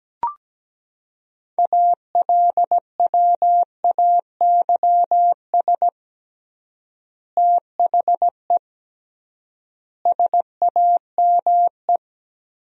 Morse code, heard as a keyed tone, about 17 wpm, 700 Hz. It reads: ALWAYS THE SAME
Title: 3x Word-Spacing - Sentences from Top 300 Words - 17wpm